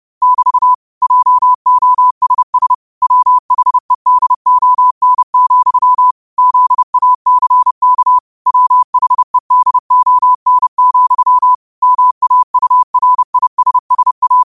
Morse code, heard as a keyed tone, about 30 wpm, 1000 Hz. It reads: <BT> JOSS WHEDON, ZACK WHEDON, MAURISSA
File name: qotdfiles / 30wpm